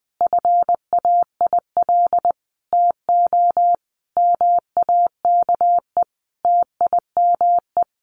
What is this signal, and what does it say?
Morse code, heard as a keyed tone, about 20 wpm, 700 Hz. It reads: FAIL TO MAKE TIME
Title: Sentences from Top 1100 Words 20wpm